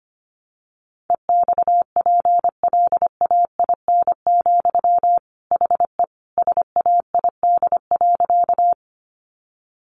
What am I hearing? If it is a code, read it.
Morse code, heard as a keyed tone, about 25 wpm, 700 Hz. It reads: EXPLAIN, HE SAID.
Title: Sentences from Top 1100 Words 25wpm